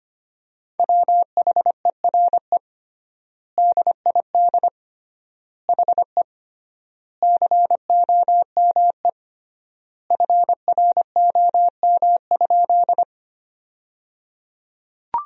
Morse code, heard as a keyed tone, about 25 wpm, 700 Hz. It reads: WHERE DID HE COME FROM?
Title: 3x Word-Spacing - Sentences from Top 200 Words - 25wpm